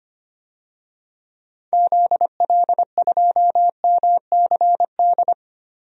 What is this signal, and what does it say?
Morse code, heard as a keyed tone, about 25 wpm, 700 Hz. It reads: ZL2MCD
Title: DXCC-100 Call Signs - 6 Characters - Repeated 2x - 25wpm